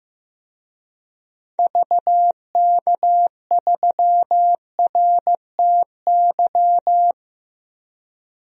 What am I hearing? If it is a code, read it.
Morse code, heard as a keyed tone, about 15 wpm, 700 Hz. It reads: VK3RTY
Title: DXCC-100 Call Signs - 6 Characters - Repeated 3x - 15wpm